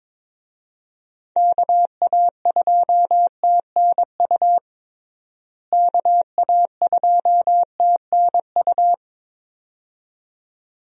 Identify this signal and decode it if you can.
Morse code, heard as a keyed tone, about 21 wpm, 700 Hz. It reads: KA2TNU KA2TNU
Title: USA Call Signs - 6 Characters - Repeated 2x - 22wpm